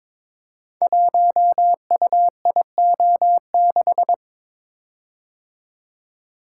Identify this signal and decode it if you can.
Morse code, heard as a keyed tone, about 21 wpm, 700 Hz. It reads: 1UIO6
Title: Code Groups - 5 Letter:Numbers - 22wpm